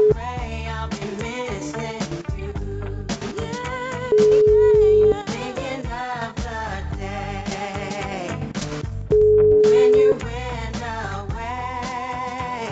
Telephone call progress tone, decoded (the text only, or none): ringback tone